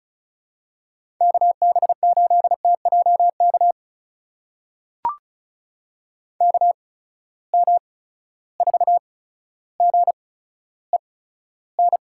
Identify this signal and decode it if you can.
Morse code, heard as a keyed tone, about 36 wpm, spaced as for 15 wpm, 700 Hz. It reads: KB8TJK KM4GEN
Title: USA Call Signs - 6 Characters - 8x Character Spacing - 35wpm